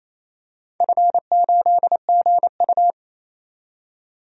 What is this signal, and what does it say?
Morse code, heard as a keyed tone, about 28 wpm, 700 Hz. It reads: F8GU
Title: DXCC-100 Call Signs - 4 Characters - 28wpm